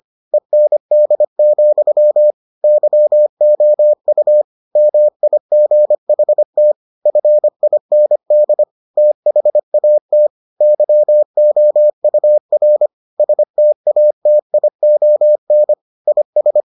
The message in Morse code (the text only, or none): END, YOU MIGHT FIND THAT YOUR STATION IS